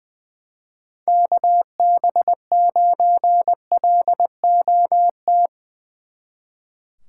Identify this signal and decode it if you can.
Morse code, heard as a keyed tone, about 22 wpm, 700 Hz. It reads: KB9LOT
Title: Speed Racing - USA Call Signs 20wpm